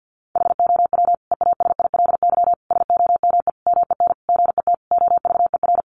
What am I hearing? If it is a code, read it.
Morse code, heard as a keyed tone, about 50 wpm, 700 Hz. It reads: HOW EASILY SOME MEN GET OVER
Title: Sentences from Top 900 Words 50wpm